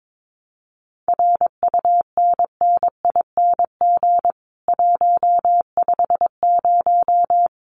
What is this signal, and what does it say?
Morse code, heard as a keyed tone, about 22 wpm, 700 Hz. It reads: RUNNING 150
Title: Common QSO Phrases 22wpm